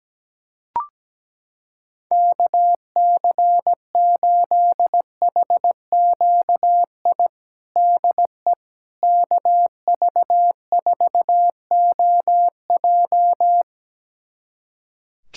Morse code, heard as a keyed tone, about 17 wpm, 700 Hz. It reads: KC8HQI DE KV4OJ
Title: USA Callsign DE Callsign 17wpm